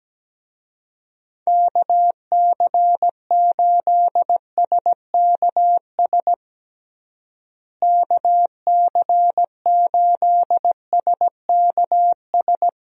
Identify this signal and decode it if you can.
Morse code, heard as a keyed tone, about 17 wpm, 700 Hz. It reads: KC8SKS KC8SKS
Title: USA Call Signs - 6 Characters - Repeated 3x - 17wpm